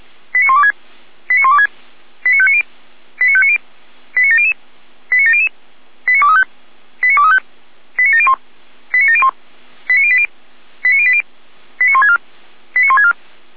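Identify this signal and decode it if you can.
Five-tone selective call (ZVEI), heard as quick runs of five tones, tones of about 70 ms. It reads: 89126 89126 89590 89590 89700 89700 89235 89235 89701 89701 89080 89080 89164 89164
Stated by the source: ZVEI five-tone call received on SDR, Feuerwehr Orlamünde fire service.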